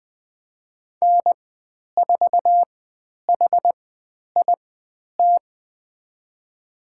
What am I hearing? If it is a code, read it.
Morse code, heard as a keyed tone, about 20 wpm, spaced as for 10 wpm, 700 Hz. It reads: N4HIT
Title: 095 Lesson 16g - Letter c - Call Signs 20wpm